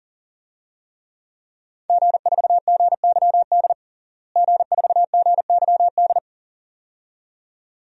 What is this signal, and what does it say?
Morse code, heard as a keyed tone, about 40 wpm, 700 Hz. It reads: G4GYD G4GYD
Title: DXCC-100 Call Signs - 5 Characters - Repeated 2x - 40wpm